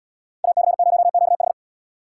Morse code, heard as a keyed tone, about 69 wpm, 700 Hz. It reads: AC0QH